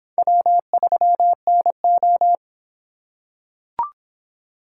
Morse code, heard as a keyed tone, about 26 wpm, 700 Hz. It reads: W3NO